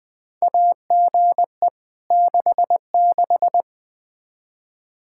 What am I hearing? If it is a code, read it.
Morse code, heard as a keyed tone, about 20 wpm, 700 Hz. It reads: AGE 66
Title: Practice QSO - Exchange 3 - More Info - 20wpm